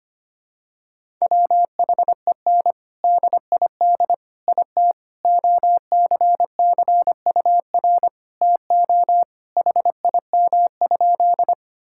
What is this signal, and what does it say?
Morse code, heard as a keyed tone, about 25 wpm, 700 Hz. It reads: WHEN DID IT OCCUR TO HIM?
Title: Sentences from Top 1000 Words - Review Part 6 - 25wpm